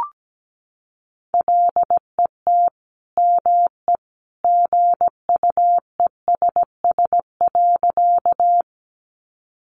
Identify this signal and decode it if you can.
Morse code, heard as a keyed tone, about 17 wpm, 700 Hz. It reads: LET ME GUESS.
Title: Sentences from Top 1500 Words 17wpm